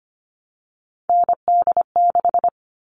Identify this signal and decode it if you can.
Morse code, heard as a keyed tone, about 25 wpm, 700 Hz. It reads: ND6
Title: All Cut Numbers - 3-Digit 25wpm